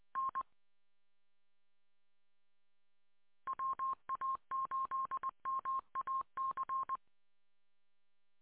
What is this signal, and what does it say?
Morse code, heard as a keyed tone, about 20 wpm, 1060 Hz. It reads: N WA8MAC